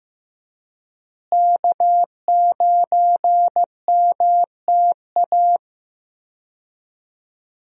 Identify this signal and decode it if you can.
Morse code, heard as a keyed tone, about 15 wpm, 700 Hz. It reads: K9MTA
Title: USA Call Signs - 5 Characters - Repeated 2x - 15wpm